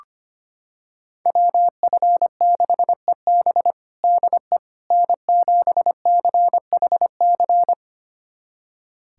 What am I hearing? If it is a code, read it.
Morse code, heard as a keyed tone, about 25 wpm, 700 Hz. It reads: WF6EB DE N7CHC
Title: USA Callsign DE Callsign - 5 Characters - 25wpm